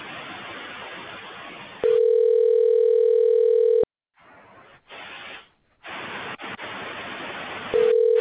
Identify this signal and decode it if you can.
Ringback tone (North America), 440 + 480 Hz.